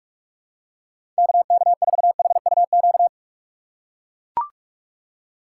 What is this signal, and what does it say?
Morse code, heard as a keyed tone, about 44 wpm, 700 Hz. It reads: KK4HUQ
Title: USA Call Signs - 6 Characters - 6x Character Spacing - 45wpm